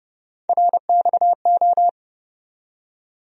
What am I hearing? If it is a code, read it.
Morse code, heard as a keyed tone, about 30 wpm, 700 Hz. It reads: RXO
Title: USA Call Sign Postfixes - 3 Letters 30wpm